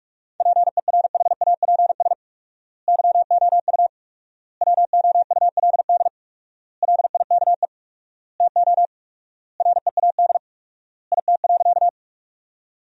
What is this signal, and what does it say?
Morse code, heard as a keyed tone, about 46 wpm, 700 Hz. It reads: PERHAPS YOU WOULD LIKE TO READ IT.
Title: 4x Word-Spacing - Sentences from Top 500 Words - 45wpm